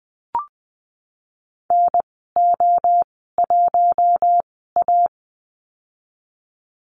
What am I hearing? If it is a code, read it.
Morse code, heard as a keyed tone, about 20 wpm, spaced as for 14 wpm, 700 Hz. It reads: NO1A
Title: 042 Lesson 8g - Number 1 - Call Signs 20wpm